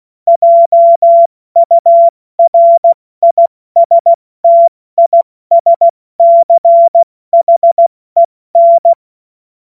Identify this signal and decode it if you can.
Morse code, heard as a keyed tone, about 16 wpm, 680 Hz. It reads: JURISTISCHEN
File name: Morsekurs / Witze / Witz 30. August 2022 70 F80